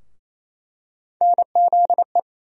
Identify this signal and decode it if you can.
Morse code, heard as a keyed tone, about 28 wpm, 700 Hz. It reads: NZE
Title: USA Call Sign Postfixes - 3 Letters 28wpm